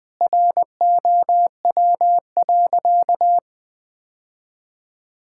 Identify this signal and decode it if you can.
Morse code, heard as a keyed tone, about 20 wpm, 700 Hz. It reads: ROW.
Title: Sentences with 7-Word Phrases - 20wpm